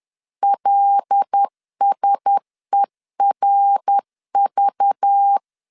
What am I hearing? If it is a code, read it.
Morse code, heard as a keyed tone, about 10 wpm, 800 Hz. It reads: LSERV
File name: LEZIONI / 6 PKX0 / 6 P 10wpm 2432